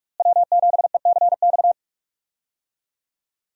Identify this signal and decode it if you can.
Morse code, heard as a keyed tone, about 46 wpm, 700 Hz. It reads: W7ECX
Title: USA Call Signs - 5 Characters - Repeated 2x - 45wpm